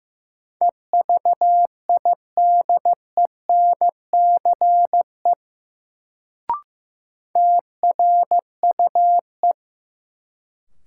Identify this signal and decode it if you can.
Morse code, heard as a keyed tone, about 15 wpm, 700 Hz. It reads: EVIDENCE TRUE
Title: Sets of 5 Words Encore - New Word Drill - 15wpm